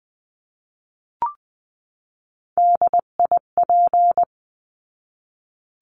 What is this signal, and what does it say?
Morse code, heard as a keyed tone, about 20 wpm, 700 Hz. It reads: DIP
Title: 259 Lesson 42d - Words 18-20wpm